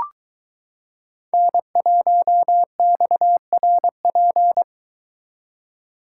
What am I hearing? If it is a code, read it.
Morse code, heard as a keyed tone, about 22 wpm, 700 Hz. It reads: N1XRP